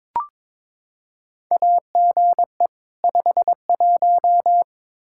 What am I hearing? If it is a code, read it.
Morse code, heard as a keyed tone, about 21 wpm, 700 Hz. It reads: AGE 51
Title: Practice QSO - Exchange 3 - More Info - 22wpm